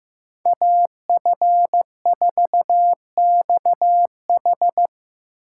Morse code, heard as a keyed tone, about 18 wpm, 700 Hz. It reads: AF4XH
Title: Speed Racing - USA 2x2 Call Signs 15wpm